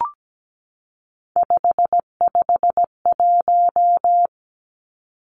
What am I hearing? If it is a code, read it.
Morse code, heard as a keyed tone, about 17 wpm, 700 Hz. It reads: 551
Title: Three Challenging Letter-Number Combinations 17wpm